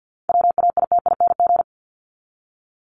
Morse code, heard as a keyed tone, about 50 wpm, 700 Hz. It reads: WAITING
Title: ING Suffix - Top 100 Words with Base Words 50wpm